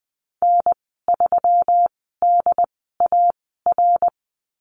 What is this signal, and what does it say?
Morse code, heard as a keyed tone, about 20 wpm, spaced as for 15 wpm, 700 Hz. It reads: N3DAR
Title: 186 Lesson 30g - Letter b - Call Signs 20wpm